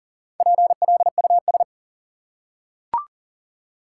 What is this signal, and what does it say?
Morse code, heard as a keyed tone, about 40 wpm, 700 Hz. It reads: PLUS
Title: Four Letter Words - 40wpm